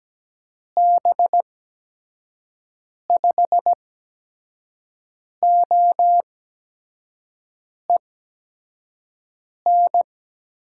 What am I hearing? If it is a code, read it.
Morse code, heard as a keyed tone, about 17 wpm, spaced as for 5 wpm, 700 Hz. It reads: B5OEN
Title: USA Call Signs - 6 Characters - 8x Character Spacing - 17wpm